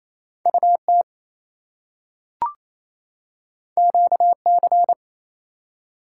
Morse code, heard as a keyed tone, about 28 wpm, 700 Hz. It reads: UT QC
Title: US States and Canadian Provinces 28wpm